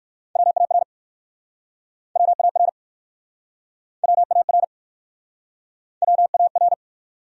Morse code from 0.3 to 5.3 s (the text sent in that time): WAR WAR WAR